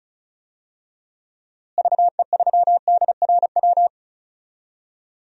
Morse code, heard as a keyed tone, about 36 wpm, 700 Hz. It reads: VE3DRW
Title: DXCC-100 Call Signs - 6 Characters - Repeated 3x - 35wpm